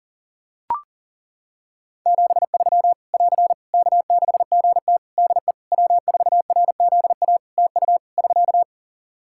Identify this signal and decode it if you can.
Morse code, heard as a keyed tone, about 37 wpm, 700 Hz. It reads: 73 <AR> K6GT DE W4RZA TU <SK>
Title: Practice QSO - Exchange 4 - Ending - 40wpm